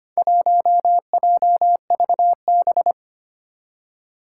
Morse code, heard as a keyed tone, about 25 wpm, 700 Hz. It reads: 1JVB
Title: Four Challenging Letter-Number Combinations 25wpm